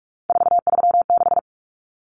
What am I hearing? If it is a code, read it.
Morse code, heard as a keyed tone, about 44 wpm, 700 Hz. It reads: 436
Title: Common Cut Numbers - 3-Digit 45wpm